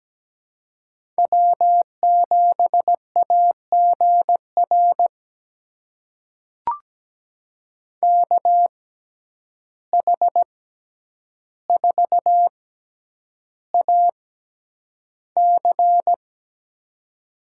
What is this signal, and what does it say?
Morse code, heard as a keyed tone, about 17 wpm, spaced as for 9 wpm, 700 Hz. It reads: W7AGR KH4AC
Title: USA Call Signs - 5 Characters - 6x Character Spacing - 17wpm